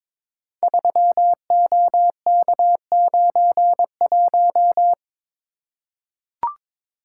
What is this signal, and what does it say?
Morse code, heard as a keyed tone, about 23 wpm, 690 Hz. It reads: 3OK91
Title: Code Groups - 5 Letter:Numbers - 22wpm